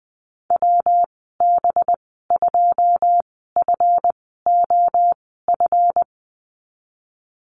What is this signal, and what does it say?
Morse code, heard as a keyed tone, about 20 wpm, spaced as for 15 wpm, 700 Hz. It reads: WB2FOF
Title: 200 Lesson 32g - Letter k - Call Signs 20wpm